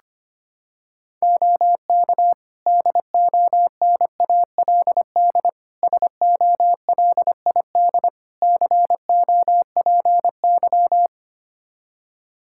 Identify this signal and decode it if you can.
Morse code, heard as a keyed tone, about 25 wpm, 700 Hz. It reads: OK DONALD SOLID COPY
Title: Practice QSO - Exchange 3 - More Info - 25wpm